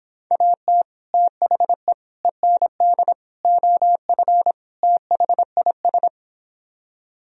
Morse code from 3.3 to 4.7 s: OF

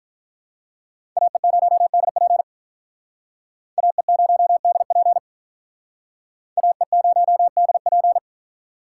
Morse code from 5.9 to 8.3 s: AE0DP

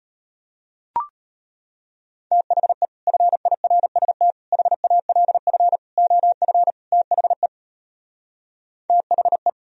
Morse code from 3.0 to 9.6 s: FIRST HALF OF THE THE

